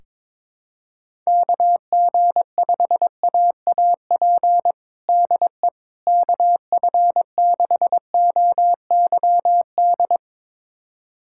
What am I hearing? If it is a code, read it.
Morse code, heard as a keyed tone, about 21 wpm, 700 Hz. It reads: KG5AAP DE KF6OYD
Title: USA Callsign DE Callsign - 6 Characters - 22wpm